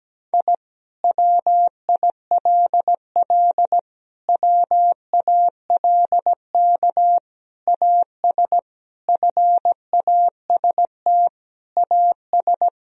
Morse code, heard as a keyed tone, about 17 wpm, 700 Hz. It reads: I WILL WALK AS FAST AS
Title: Sentences from Top 1000 Words - Review Part 2 - 17wpm